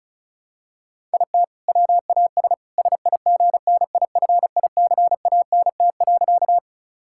Morse code, heard as a keyed tone, about 36 wpm, 700 Hz. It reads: IT WAS SIGNIFICANT.
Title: Sentences from Top 1000 Words - Review Part 6 - 35wpm